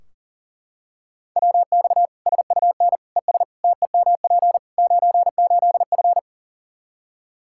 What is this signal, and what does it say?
Morse code, heard as a keyed tone, about 40 wpm, 700 Hz. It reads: WX SUN ES TEMP 98F
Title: K1USN SST QSO Practice 40wpm